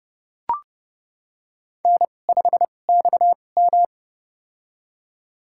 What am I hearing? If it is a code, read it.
Morse code, heard as a keyed tone, about 32 wpm, spaced as for 22 wpm, 700 Hz. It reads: N5XM
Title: USA Call Signs - 2x Character Spacing - 30wpm